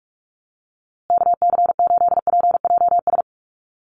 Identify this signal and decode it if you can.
Morse code, heard as a keyed tone, about 44 wpm, 700 Hz. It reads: KC8PJS
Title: USA Call Signs Repeated 3x 45wpm